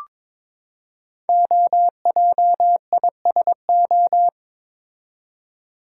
Morse code, heard as a keyed tone, about 22 wpm, 700 Hz. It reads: OJISO
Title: Code Groups - 5 Letters - 22wpm